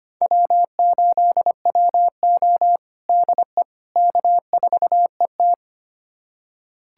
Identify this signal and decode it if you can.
Morse code, heard as a keyed tone, about 25 wpm, 700 Hz. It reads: W8WO DE K4ET
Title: USA Callsign DE Callsign - 4 Characters - 25wpm